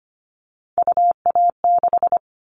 Morse code, heard as a keyed tone, about 25 wpm, 700 Hz. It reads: UA6
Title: Callsign Prefixes for Top 100 DXCC - 25wpm